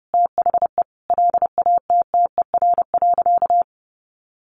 Morse code, heard as a keyed tone, about 30 wpm, 700 Hz. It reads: THE LATTER.
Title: A Sound Of Thunder 30wpm